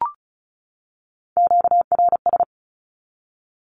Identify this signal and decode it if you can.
Morse code, heard as a keyed tone, about 34 wpm, 700 Hz. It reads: QRS
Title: Q-Codes 35wpm